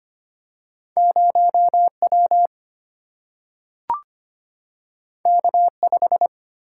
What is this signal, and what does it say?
Morse code, heard as a keyed tone, about 25 wpm, 700 Hz. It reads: 0 W K 5
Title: Single-Letter Single-Number 25wpm